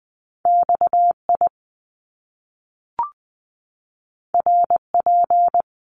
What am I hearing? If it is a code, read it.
Morse code, heard as a keyed tone, about 20 wpm, 700 Hz. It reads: XI RP
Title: Top Two Letter Combinations in Words 20wpm